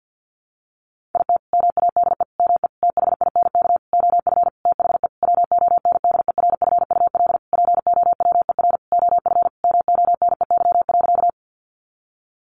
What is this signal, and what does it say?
Morse code, heard as a keyed tone, about 48 wpm, 700 Hz. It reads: IT MADE ME THINK OF THE WONDERFUL POWER OF MONEY.
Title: Sentences from Top 1100 Words 50wpm